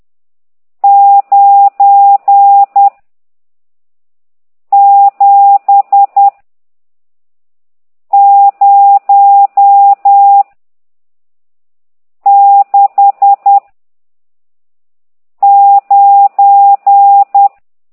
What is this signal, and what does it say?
Morse code, heard as a keyed tone, about 10 wpm, 800 Hz. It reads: 9 7 0 6 9
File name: ct2idw / exerc14